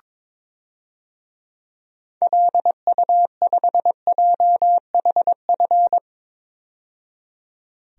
Morse code, heard as a keyed tone, about 22 wpm, 700 Hz. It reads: LU5JHF
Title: DXCC-100 Call Signs - 6 Characters - 22wpm